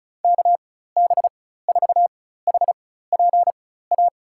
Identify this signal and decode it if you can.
Morse code, heard as a keyed tone, about 34 wpm, spaced as for 16 wpm, 700 Hz. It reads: KB4HPA
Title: USA Call Signs - 4x Character Spacing - 35wpm